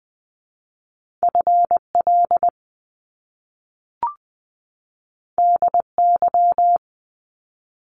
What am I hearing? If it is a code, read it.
Morse code, heard as a keyed tone, about 20 wpm, 700 Hz. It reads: FL DY
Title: Top Two Letter Combinations in Words 20wpm